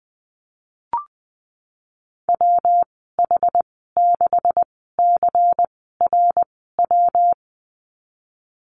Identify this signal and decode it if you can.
Morse code, heard as a keyed tone, about 19 wpm, spaced as for 14 wpm, 700 Hz. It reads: WH6CRW
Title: 167 Lesson 27g - Number 7 - Call Signs 20wpm